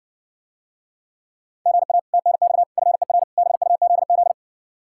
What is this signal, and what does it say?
Morse code, heard as a keyed tone, about 60 wpm, 700 Hz. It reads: GA TNX FER BUZZ